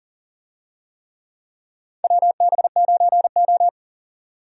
Morse code, heard as a keyed tone, about 39 wpm, 700 Hz. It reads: WB9O
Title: USA Call Signs - 4 Characters - 40wpm